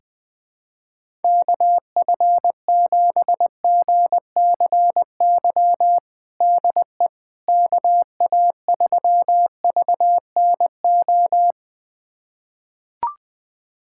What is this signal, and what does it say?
Morse code, heard as a keyed tone, about 20 wpm, 700 Hz. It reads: KF7GCY DE KA3VNO
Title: USA Callsign DE Callsign - 6 Characters - 20wpm